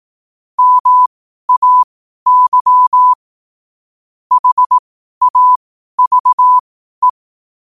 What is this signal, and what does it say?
Morse code, heard as a keyed tone, about 18 wpm, spaced as for 12 wpm, 1000 Hz. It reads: MAY HAVE
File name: qotdfiles / 12wpm